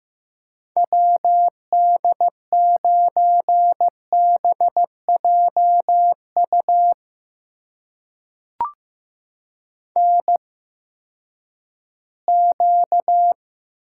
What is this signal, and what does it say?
Morse code, heard as a keyed tone, about 15 wpm, spaced as for 10 wpm, 700 Hz. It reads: WD9BJU NQ
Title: USA Call Signs - 8x Character Spacing - 15wpm